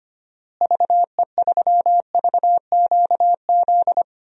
Morse code, heard as a keyed tone, about 25 wpm, 700 Hz. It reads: VE3VQZ
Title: DXCC-100 Call Signs - 6 Characters - Repeated 2x - 25wpm